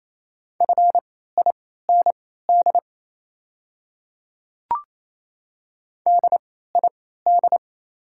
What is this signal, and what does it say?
Morse code, heard as a keyed tone, about 28 wpm, spaced as for 16 wpm, 690 Hz. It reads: FIND DID
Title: 3x Character Spacing - Top 100 Words - 28wpm